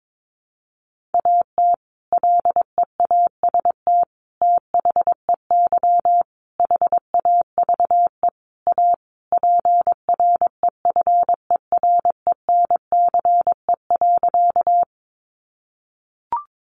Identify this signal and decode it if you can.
Morse code, heard as a keyed tone, about 22 wpm, 700 Hz. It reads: AT LEAST THEY HAVE A PREFERENCE.
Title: Sentences from Top 1500 Words 22wpm